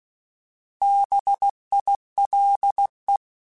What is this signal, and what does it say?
Morse code, heard as a keyed tone, about 16 wpm, 800 Hz. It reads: BILE
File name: Alphabetical / Alphabetical mp3s / Bs @15